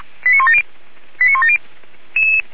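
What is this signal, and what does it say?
Five-tone selective call (ZVEI), heard as quick runs of five tones, tones of about 70 ms. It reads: 89170 89170
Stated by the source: ZVEI five-tone call received on an SDR, logged by Feuerwehr Orlamünde.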